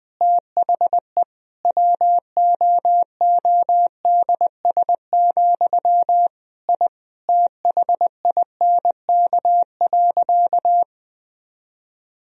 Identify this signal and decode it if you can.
Morse code, heard as a keyed tone, about 20 wpm, 700 Hz. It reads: THE WOODS, I THINK.
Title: Sentences from Top 1000 Words - Review Part 2 - 20wpm